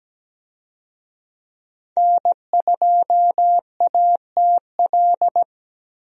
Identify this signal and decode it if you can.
Morse code, heard as a keyed tone, about 17 wpm, 700 Hz. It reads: N2ATL
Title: USA Call Signs - 5 Characters - Repeated 3x - 17wpm